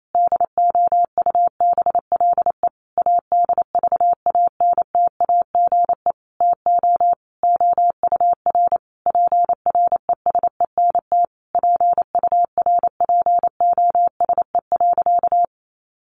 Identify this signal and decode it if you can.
Morse code, heard as a keyed tone, about 28 wpm, 700 Hz. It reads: DOUBLE ADVANTAGE TO OUR PRESENT PURPOSE.
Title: Sentences from Top 1400 Words 28wpm